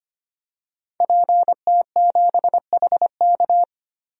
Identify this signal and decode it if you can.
Morse code, heard as a keyed tone, about 25 wpm, 700 Hz. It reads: PT7HK
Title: DXCC-100 Call Signs - 5 Characters - 25wpm